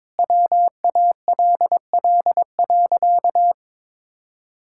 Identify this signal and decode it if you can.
Morse code, heard as a keyed tone, about 22 wpm, 700 Hz. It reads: WALL.
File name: practice / 22wpm / Sentences with 2-Word Phrases - 22wpm